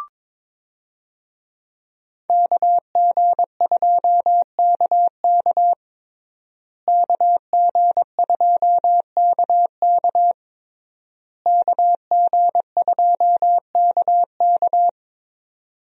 Morse code, heard as a keyed tone, about 22 wpm, 700 Hz. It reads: KG2KK KG2KK KG2KK
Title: USA Call Signs - 5 Characters - Repeated 3x - 22wpm